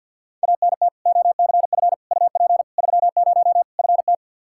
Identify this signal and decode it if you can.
Morse code, heard as a keyed tone, about 46 wpm, 700 Hz. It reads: ANT OCF UP 30 FT
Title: Practice QSO - Exchange 2 - Info - 50wpm